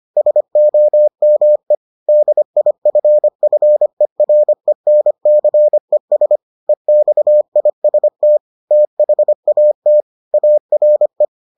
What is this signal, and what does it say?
Morse code, heard as a keyed tone, about 25 wpm, 600 Hz. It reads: SOME DIFFERENCES EXIST THAT ARE